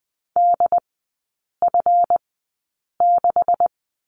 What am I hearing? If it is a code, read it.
Morse code, heard as a keyed tone, about 20 wpm, spaced as for 8 wpm, 700 Hz. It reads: DF6
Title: 152 Lesson 25e - Letter p - Three Characters 20wpm